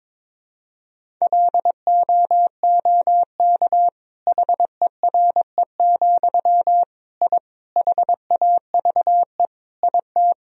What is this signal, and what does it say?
Morse code, heard as a keyed tone, about 22 wpm, 700 Hz. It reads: LOOK HERE, I HAVE IT
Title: Sentences from Top 1000 Words - Review Part 1 - 22wpm